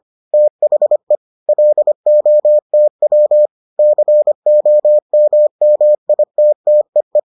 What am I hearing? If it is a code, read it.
Morse code, heard as a keyed tone, about 25 wpm, 600 Hz. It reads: THE LOTW COMMITTEE